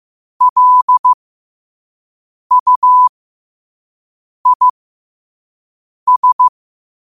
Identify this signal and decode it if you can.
Morse code, heard as a keyed tone, about 15 wpm, spaced as for 5 wpm, 1000 Hz. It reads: LUIS
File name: qotdfiles / 5wpm